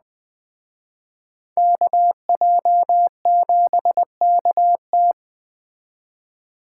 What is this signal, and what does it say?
Morse code, heard as a keyed tone, about 20 wpm, 700 Hz. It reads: KJ7KT
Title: Speed Racing - USA 2x2 Call Signs 17wpm